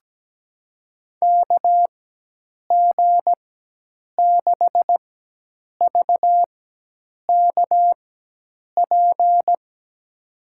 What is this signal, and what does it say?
Morse code, heard as a keyed tone, about 17 wpm, spaced as for 8 wpm, 700 Hz. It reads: KG6VKP
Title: USA Call Signs - 6 Characters - 4x Character Spacing - 17wpm